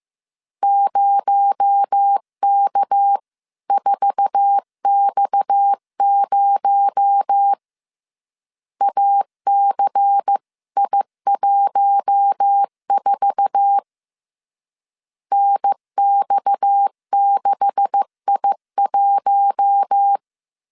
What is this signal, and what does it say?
Morse code, heard as a keyed tone, about 15 wpm, 800 Hz. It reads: 0K4X0 ACI14 NX6I1
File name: CW / DET / 1 9 1500C S A 14wpm 3222